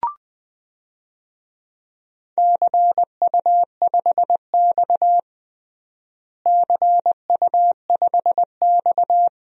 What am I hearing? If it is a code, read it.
Morse code, heard as a keyed tone, about 21 wpm, 700 Hz. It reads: CU5X CU5X
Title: DXCC-100 Call Signs - 4 Characters - Repeated 2x - 20wpm